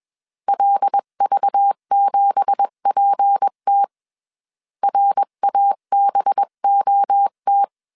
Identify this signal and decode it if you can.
Morse code, heard as a keyed tone, about 21 wpm, 800 Hz. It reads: LV7PT RABOT
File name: CW / DET / 1 9 1500C S B 20wpm 2313